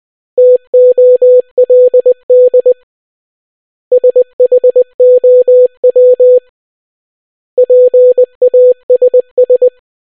Morse code, heard as a keyed tone, about 20 wpm, 500 Hz. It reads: TOLD SHOW PASS